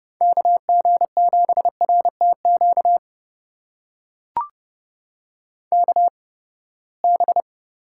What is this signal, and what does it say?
Morse code, heard as a keyed tone, about 29 wpm, spaced as for 21 wpm, 700 Hz. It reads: KG7RTQ KB
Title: USA Call Signs - 6 Characters - 8x Character Spacing - 30wpm